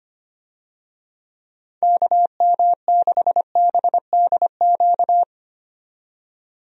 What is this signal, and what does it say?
Morse code, heard as a keyed tone, about 25 wpm, 700 Hz. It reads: KM6BDQ
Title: USA Call Signs - 6 Characters - 25wpm